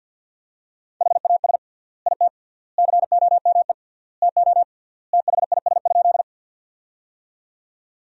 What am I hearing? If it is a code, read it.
Morse code, heard as a keyed tone, about 50 wpm, 700 Hz. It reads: HAS IT COME TO THIS?
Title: 3x Word-Spacing - Sentences from Top 100 Words - 50wpm